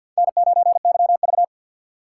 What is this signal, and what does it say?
Morse code, heard as a keyed tone, about 47 wpm, 700 Hz. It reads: N9YV
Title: USA Call Signs - 3x Character Spacing - 50wpm